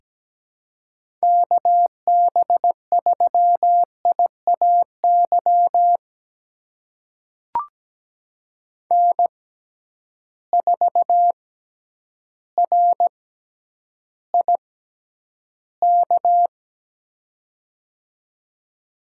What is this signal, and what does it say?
Morse code, heard as a keyed tone, about 17 wpm, spaced as for 9 wpm, 700 Hz. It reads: KB3IAY N4RIK
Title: USA Call Signs - 6x Character Spacing - 17wpm